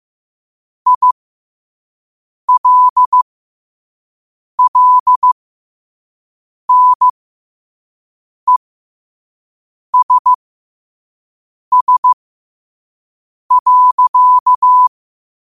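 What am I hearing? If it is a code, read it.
Morse code, heard as a keyed tone, about 15 wpm, spaced as for 5 wpm, 1000 Hz. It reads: ILLNESS.
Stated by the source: Morse code practice at 5 wpm.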